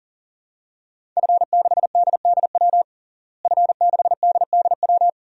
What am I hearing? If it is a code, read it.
Morse code, heard as a keyed tone, about 40 wpm, 700 Hz. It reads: F6DDW F6DDW
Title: DXCC-100 Call Signs - 5 Characters - Repeated 2x - 40wpm